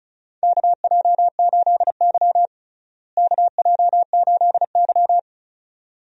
Morse code, heard as a keyed tone, about 36 wpm, 700 Hz. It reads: KJ8Y KJ8Y
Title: USA Call Signs - 4 Characters - Repeated 2x - 35wpm